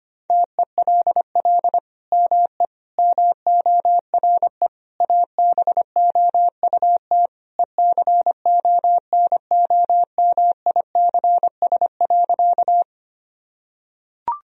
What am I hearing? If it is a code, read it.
Morse code, heard as a keyed tone, about 25 wpm, 700 Hz. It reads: TELL ME MORE ABOUT ECONOMICS.
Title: Sentences from Top 1000 Words - Review Part 3 - 25wpm